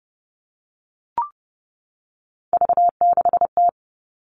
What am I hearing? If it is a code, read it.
Morse code, heard as a keyed tone, about 29 wpm, 700 Hz. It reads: V6T
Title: All Cut Numbers - 3-Digit 30wpm